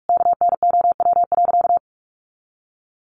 Morse code, heard as a keyed tone, about 44 wpm, 700 Hz. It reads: KNOW.
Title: Sentences from Top 1200 Words 45wpm